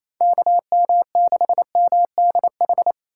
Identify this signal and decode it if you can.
Morse code, heard as a keyed tone, about 28 wpm, 700 Hz. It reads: KM6MDH
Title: USA Call Signs - 6 Characters - 6x Character Spacing - 28wpm